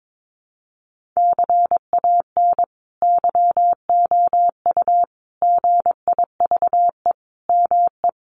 Morse code, heard as a keyed tone, about 22 wpm, 700 Hz. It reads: CAN YOU GIVE ME
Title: Sentences from Top 1500 Words 22wpm